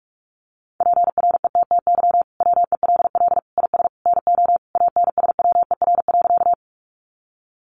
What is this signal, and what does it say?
Morse code, heard as a keyed tone, about 44 wpm, 700 Hz. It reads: PRETTY WELL IS NO ANSWER.
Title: Sentences from Top 1000 Words 45wpm